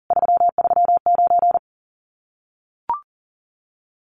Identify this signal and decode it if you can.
Morse code, heard as a keyed tone, about 38 wpm, 700 Hz. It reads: 339
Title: Common QSO Phrases Encore 40wpm